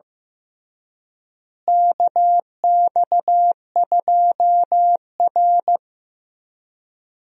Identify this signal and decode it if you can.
Morse code, heard as a keyed tone, about 15 wpm, 700 Hz. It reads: KX2R
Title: USA Call Signs - 4 Characters - Repeated 3x - 15wpm